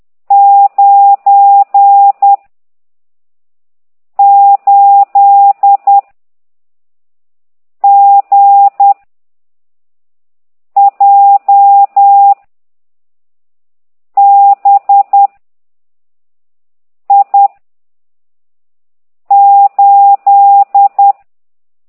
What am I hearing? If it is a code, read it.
Morse code, heard as a keyed tone, about 10 wpm, 800 Hz. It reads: 9 8 G J B I 8
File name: ct2idw / exerc15